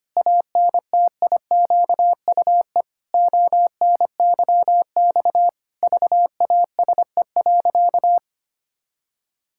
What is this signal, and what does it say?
Morse code, heard as a keyed tone, about 25 wpm, 700 Hz. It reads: ANTIQUE ONYX VASE.
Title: Pangram Challenge v2 - 25wpm